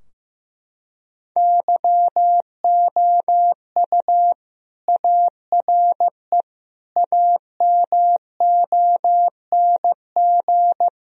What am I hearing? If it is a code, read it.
Morse code, heard as a keyed tone, about 15 wpm, 700 Hz. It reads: YOU ARE AMONG
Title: Sentences from Top 1000 Words - Review Part 3 - 15wpm